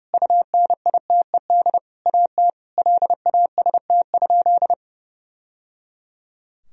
Morse code, heard as a keyed tone, about 30 wpm, 700 Hz. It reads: UNITED AT LAST?
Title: Sentences from Top 1000 Words - Review Part 4 - 30wpm